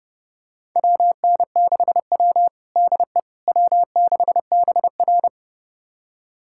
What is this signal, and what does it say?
Morse code, heard as a keyed tone, about 30 wpm, 700 Hz. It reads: WN6W DE W6BR
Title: USA Callsign DE Callsign - 4 Characters - 30wpm